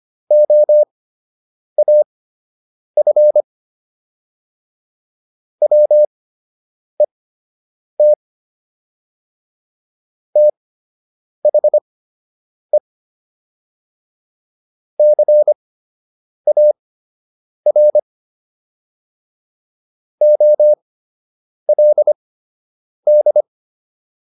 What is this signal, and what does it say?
Morse code, heard as a keyed tone, about 25 wpm, spaced as for 8 wpm, 600 Hz. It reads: OAF WET THE CAR OLD